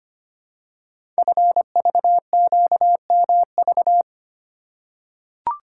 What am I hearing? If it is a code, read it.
Morse code, heard as a keyed tone, about 25 wpm, 700 Hz. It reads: FVQMV
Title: Code Groups - 5 Letter:Numbers - 25wpm